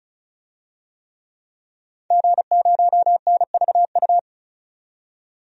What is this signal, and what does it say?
Morse code, heard as a keyed tone, about 34 wpm, 700 Hz. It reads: G0NVU
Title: DXCC-100 Call Signs - 5 Characters - 35wpm